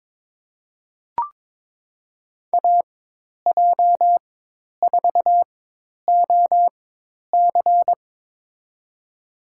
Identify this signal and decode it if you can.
Morse code, heard as a keyed tone, about 22 wpm, spaced as for 10 wpm, 700 Hz. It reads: AJ4OC
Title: USA Call Signs - 5 Characters - 4x Character Spacing - 22wpm